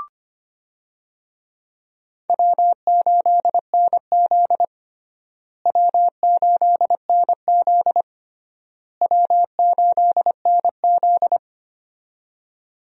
Morse code, heard as a keyed tone, about 25 wpm, 700 Hz. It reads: W8NZ W8NZ W8NZ
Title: USA Call Signs - 4 Characters - Repeated 3x - 25wpm